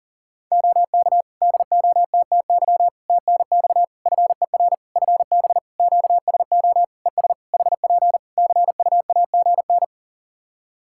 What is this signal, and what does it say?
Morse code, heard as a keyed tone, about 37 wpm, 700 Hz. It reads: OK DOTTY TNX FER FB QSO ES HP CUAGN
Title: K1USN SST QSO Practice 40wpm